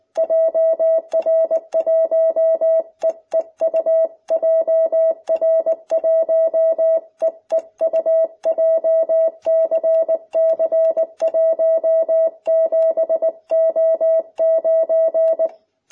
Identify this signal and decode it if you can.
Morse code, heard as a keyed tone, about 19 wpm, 660 Hz. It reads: JR1EEU JR1EEU JCC17O9